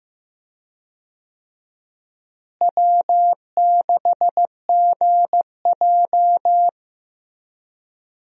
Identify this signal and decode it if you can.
Morse code, heard as a keyed tone, about 15 wpm, 700 Hz. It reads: W6GJ
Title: USA Call Signs - 4 Characters - Repeated 2x - 15wpm